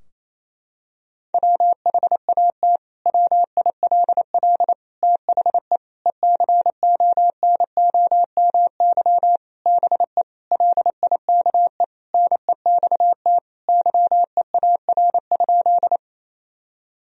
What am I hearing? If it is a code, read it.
Morse code, heard as a keyed tone, about 28 wpm, 700 Hz. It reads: WHAT WILL THE ECONOMY BE LIKE NEXT YEAR?
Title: Sentences from Top 1000 Words - Review Part 3 - 28wpm